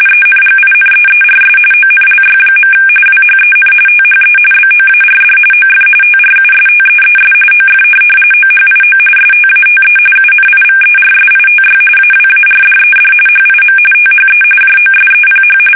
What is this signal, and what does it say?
RTTY, about 45.45 baud. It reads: ENGINEERS ONLY. WHEN THE BOARD OF TRUSTEES OF OHIO STATE UNIVERSITY AWARDED HIM THE JOSEPH
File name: fsk850 / benjamin-g-lamme